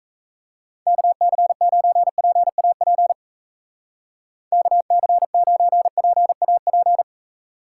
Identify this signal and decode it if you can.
Morse code, heard as a keyed tone, about 34 wpm, 700 Hz. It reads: KC9PAP KC9PAP
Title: Speed Racing - USA 2x3 Call Signs 28wpm